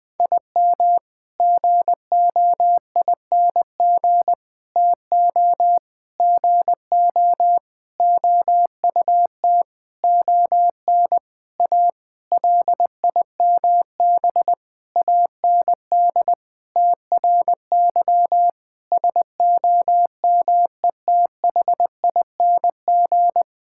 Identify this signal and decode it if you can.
Morse code, heard as a keyed tone, about 20 wpm, 700 Hz. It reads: IM GOING TO GO OUT ON A LIMB AND TRY SOMETHING
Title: Sentences with 8-Word Phrases - 20wpm